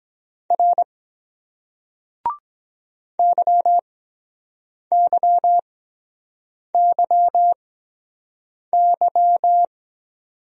Morse code, heard as a keyed tone, about 23 wpm, spaced as for 7 wpm, 700 Hz. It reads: R YYYY